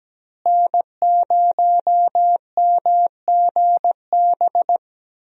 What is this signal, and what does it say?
Morse code, heard as a keyed tone, about 17 wpm, 700 Hz. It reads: N0MGB
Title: USA Call Signs - 5 Characters - Repeated 3x - 17wpm